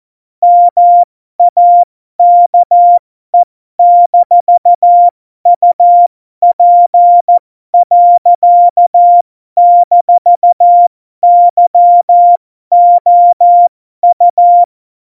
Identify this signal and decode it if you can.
Morse code, heard as a keyed tone, about 14 wpm, spaced as for 12 wpm, 700 Hz. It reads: MAKE-UP.-YOU
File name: Morsekurs / Witze